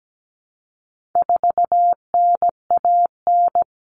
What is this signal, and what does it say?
Morse code, heard as a keyed tone, about 17 wpm, 700 Hz. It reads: 4NAN
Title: Common Cut Numbers - 4-Digit 17wpm